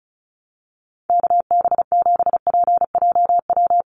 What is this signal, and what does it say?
Morse code, heard as a keyed tone, about 34 wpm, 700 Hz. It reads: KB7PJW
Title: USA Call Signs 35wpm